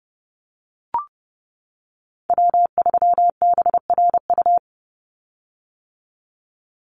Morse code, heard as a keyed tone, about 30 wpm, 700 Hz. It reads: W3BRU
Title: USA Call Signs 30wpm